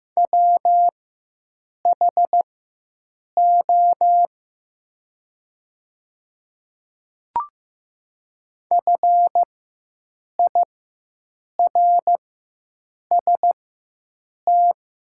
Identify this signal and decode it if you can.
Morse code, heard as a keyed tone, about 15 wpm, spaced as for 7 wpm, 700 Hz. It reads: WHO FIRST
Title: 4x Character Spacing - Top 300 Words - 15wpm